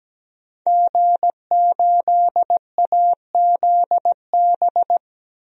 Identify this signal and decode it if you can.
Morse code, heard as a keyed tone, about 17 wpm, 700 Hz. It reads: G8AZB
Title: DXCC-100 Call Signs - 5 Characters - 17wpm